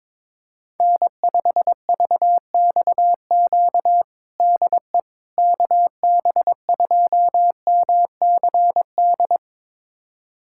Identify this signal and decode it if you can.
Morse code, heard as a keyed tone, about 21 wpm, 700 Hz. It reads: N5VXQ DE KB2MCD
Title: USA Callsign DE Callsign 22wpm